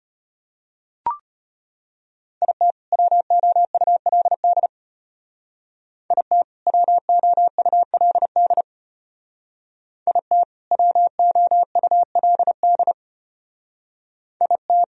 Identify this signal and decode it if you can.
Morse code, heard as a keyed tone, about 31 wpm, 700 Hz. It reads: IT WOULD IT WOULD IT WOULD IT
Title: Speed-Racing - Sets of 2 Words 25wpm